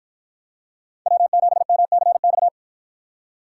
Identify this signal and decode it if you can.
Morse code, heard as a keyed tone, about 49 wpm, 700 Hz. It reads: W7MCX